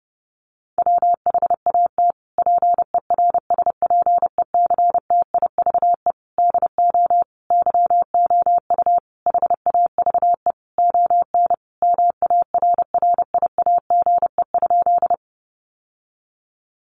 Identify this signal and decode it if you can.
Morse code, heard as a keyed tone, about 30 wpm, 700 Hz. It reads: WHAT PERSPECTIVE DO YOU HAVE ON MARRIAGE?
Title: Sentences from Top 1200 Words 30wpm